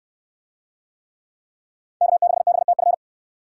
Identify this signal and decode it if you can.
Morse code, heard as a keyed tone, about 69 wpm, 700 Hz. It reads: K6BTF